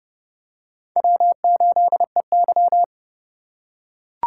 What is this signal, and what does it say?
Morse code, heard as a keyed tone, about 30 wpm, 700 Hz. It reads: W8EY
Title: USA Call Signs - 4 Characters - 6x Character Spacing - 30wpm